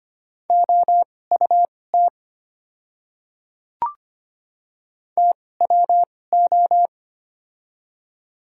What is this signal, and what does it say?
Morse code, heard as a keyed tone, about 25 wpm, spaced as for 18 wpm, 690 Hz. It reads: OUT TWO
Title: 2x Character Spacing - Top 100 Words - 25wpm